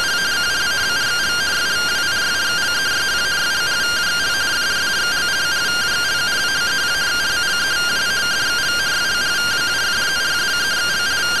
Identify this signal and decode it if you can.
Two-tone teleprinter signal, about 170 Hz shift.
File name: findings / MR CORONAL fr1 -131.gen-20 rtty rx